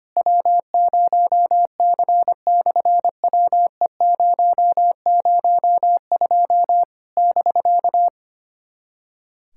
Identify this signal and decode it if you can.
Morse code, heard as a keyed tone, about 25 wpm, 700 Hz. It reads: W0C/WE002 <BK>
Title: SOTA QSO Practice 25wpm